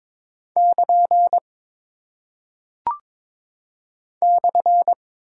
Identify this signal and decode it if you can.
Morse code, heard as a keyed tone, about 22 wpm, 700 Hz. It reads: <KN> /
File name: practice / 22wpm / Single Punctuation-Prosign - 22wpm